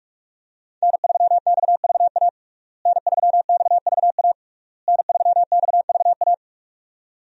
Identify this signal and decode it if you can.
Morse code, heard as a keyed tone, about 46 wpm, 700 Hz. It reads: N3XVA N3XVA N3XVA
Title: USA Call Signs - 5 Characters - Repeated 3x - 45wpm